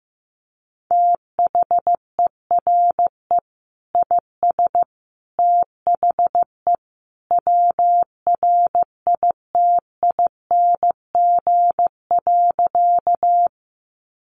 Morse code, heard as a keyed tone, about 14 wpm, 700 Hz. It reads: THERE IS THE WRITING.
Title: Sentences from Top 800 Words 15wpm